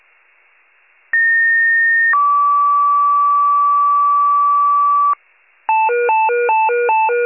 Two-tone page, heard as a one-second tone, then a three-second tone, about 1850 then 1150 Hz. It is heard from 1.1 to 5.1 s.
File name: wcalert